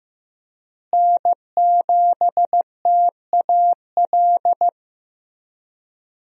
Morse code, heard as a keyed tone, about 15 wpm, 700 Hz. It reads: N7TAL
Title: USA Call Signs - 5 Characters - Repeated 2x - 15wpm